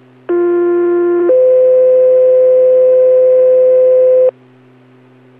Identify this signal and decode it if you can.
Two-tone page, heard as a one-second tone, then a three-second tone, about 350 then 510 Hz. It is heard from 0.3 to 4.3 s.